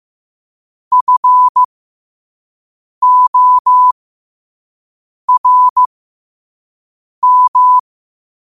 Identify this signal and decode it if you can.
Morse code, heard as a keyed tone, about 15 wpm, spaced as for 5 wpm, 1000 Hz. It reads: FORM
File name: qotdfiles / 5wpm